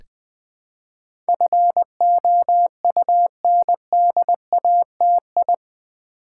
Morse code, heard as a keyed tone, about 20 wpm, 700 Hz. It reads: FOUNDATI
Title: Build A Word - 4 Letters At A Time - 10-Letter Words - 20wpm